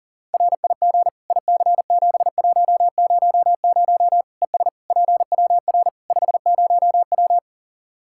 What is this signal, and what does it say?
Morse code, heard as a keyed tone, about 37 wpm, 700 Hz. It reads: RIG IC7100 ES PWR 50W
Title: Practice QSO - Exchange 2 - Info - 40wpm